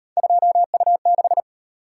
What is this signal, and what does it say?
Morse code, heard as a keyed tone, about 38 wpm, 700 Hz. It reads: 2U6